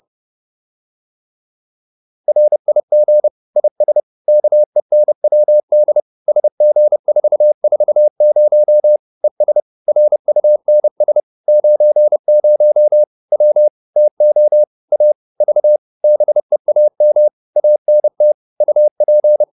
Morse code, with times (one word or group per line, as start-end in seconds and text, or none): 2.3-3.3 RIG
3.6-4.0 IS
4.3-6.0 KENWD
6.3-9.0 SG440
9.2-9.6 ES
9.9-11.2 RUNS
11.5-13.0 90
13.3-13.7 W
14.0-14.6 TO
14.9-15.1 A
15.4-15.8 V
16.0-17.3 BEAM
17.6-18.3 ANT
18.6-19.4 UP